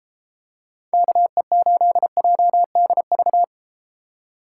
Morse code, heard as a keyed tone, about 30 wpm, 700 Hz. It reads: KE8JDV